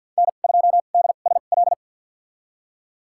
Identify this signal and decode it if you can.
Morse code, heard as a keyed tone, about 53 wpm, spaced as for 37 wpm, 700 Hz. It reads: N2DSL
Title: USA Call Signs - 5 Characters - 2x Character Spacing - 50wpm